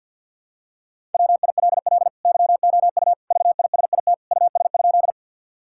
Morse code, heard as a keyed tone, about 50 wpm, 700 Hz. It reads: WILL YOU VISIT US?
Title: Sentences from Top 1000 Words - Review Part 6 - 50wpm